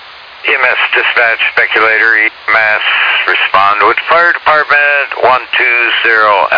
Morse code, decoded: none

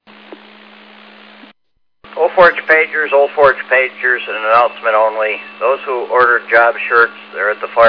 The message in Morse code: none

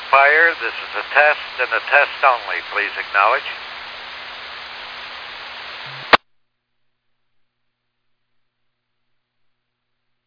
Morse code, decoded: none

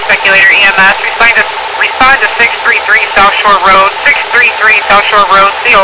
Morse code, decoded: none

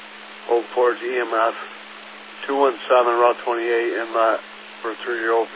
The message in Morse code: none

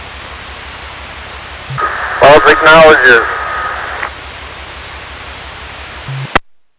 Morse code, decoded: none

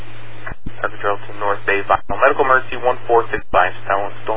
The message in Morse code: none

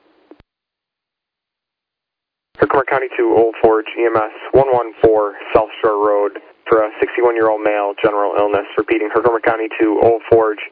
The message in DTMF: none

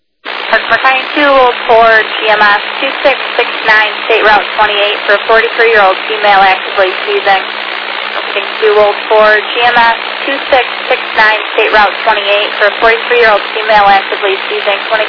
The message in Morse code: none

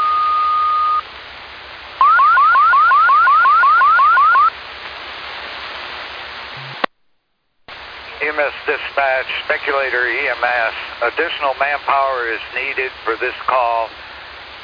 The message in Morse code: none

